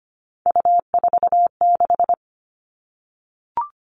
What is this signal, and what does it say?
Morse code, heard as a keyed tone, about 25 wpm, 700 Hz. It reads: U46
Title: Three Challenging Letter-Number Combinations 25wpm